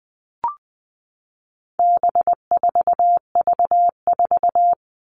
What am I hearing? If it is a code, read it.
Morse code, heard as a keyed tone, about 20 wpm, 700 Hz. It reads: B4V4
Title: All Cut Numbers - 4-Digit 20wpm